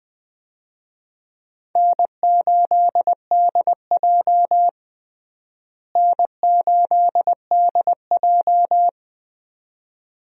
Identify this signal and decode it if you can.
Morse code, heard as a keyed tone, about 20 wpm, 700 Hz. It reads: N8DJ N8DJ
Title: USA Call Signs - 4 Characters - Repeated 2x - 20wpm